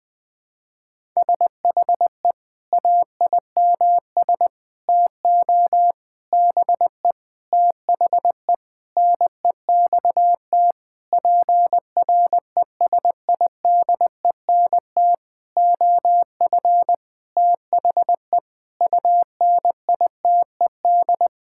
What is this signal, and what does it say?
Morse code, heard as a keyed tone, about 20 wpm, 700 Hz. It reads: SHE AIMS TO BE THE NEXT PRESIDENT OF THE UNITED
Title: Sentences with 9-Word Phrases - Spoken Beginning - 20wpm